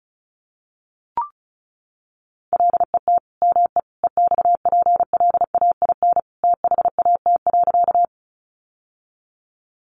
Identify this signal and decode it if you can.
Morse code, heard as a keyed tone, about 34 wpm, 700 Hz. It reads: LET ME EXPLAIN THAT.
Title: Sentences from Top 1100 Words 35wpm